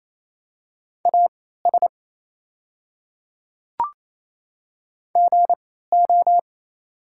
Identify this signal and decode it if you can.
Morse code, heard as a keyed tone, about 28 wpm, spaced as for 16 wpm, 700 Hz. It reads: AS GO
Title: 3x Character Spacing - Top 100 Words - 28wpm